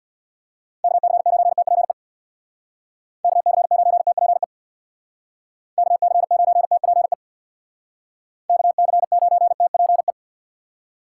Morse code, heard as a keyed tone, about 65 wpm, 700 Hz. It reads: KC9TPE KC9TPE KC9TPE KC9TPE